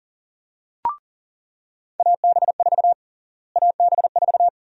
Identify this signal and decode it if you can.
Morse code, heard as a keyed tone, about 37 wpm, 700 Hz. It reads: AB4 AB4
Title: USA Call Sign Prefixes - Repeated 2x - 40wpm